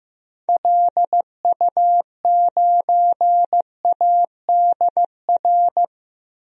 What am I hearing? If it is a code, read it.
Morse code, heard as a keyed tone, about 15 wpm, 700 Hz. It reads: LU9ADR
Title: DXCC-100 Call Signs - 6 Characters - Repeated 2x - 15wpm